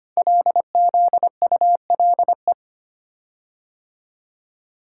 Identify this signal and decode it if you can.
Morse code, heard as a keyed tone, about 25 wpm, 700 Hz. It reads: LZULE
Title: Code Groups - 5 Letters - 25wpm